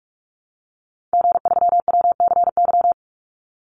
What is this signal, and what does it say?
Morse code, heard as a keyed tone, about 44 wpm, 700 Hz. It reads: G3WCY
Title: Call Signs Repeated 3x - 45wpm